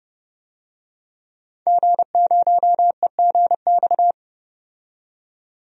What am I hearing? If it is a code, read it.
Morse code, heard as a keyed tone, about 31 wpm, 700 Hz. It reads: G0EGX
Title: DXCC-100 Call Signs - 5 Characters - 30wpm